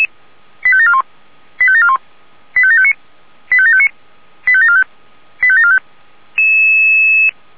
Five-tone selective call (ZVEI), heard as quick runs of five tones, tones of about 70 ms. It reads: 85741 85741 85759 85759 85745 85745